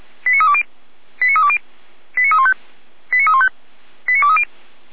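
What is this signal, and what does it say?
Five-tone selective call (ZVEI), heard as quick runs of five tones, tones of about 70 ms. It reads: 89320 89320 89315 89315 89230
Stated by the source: ZVEI five-tone call received via SDR, archived by Feuerwehr Orlamünde (fire brigade).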